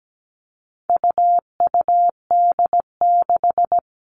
Morse code, heard as a keyed tone, about 17 wpm, 700 Hz. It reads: UUD6
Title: All Cut Numbers - 4-Digit 17wpm